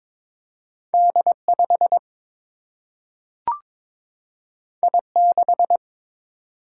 Morse code, heard as a keyed tone, about 21 wpm, 700 Hz. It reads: D 5 I 6
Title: Single-Letter Single-Number 22wpm